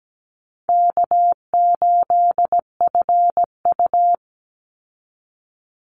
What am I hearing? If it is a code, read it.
Morse code, heard as a keyed tone, about 17 wpm, 700 Hz. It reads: K8FU
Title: Easy Call Signs Repeated 3x - 17wpm